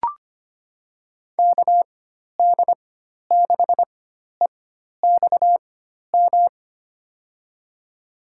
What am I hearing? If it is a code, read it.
Morse code, heard as a keyed tone, about 25 wpm, spaced as for 12 wpm, 700 Hz. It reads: KD6EXM
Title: USA Call Signs - 4x Character Spacing - 25wpm